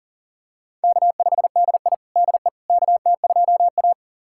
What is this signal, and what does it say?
Morse code, heard as a keyed tone, about 40 wpm, 700 Hz. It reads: K5DI DE KT2A
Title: USA Callsign DE Callsign - 4 Characters - 40wpm